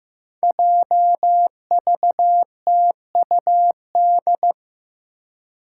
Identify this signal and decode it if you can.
Morse code, heard as a keyed tone, about 15 wpm, 700 Hz. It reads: JVTUD
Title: Five Challenging Letter-Number Combinations 15wpm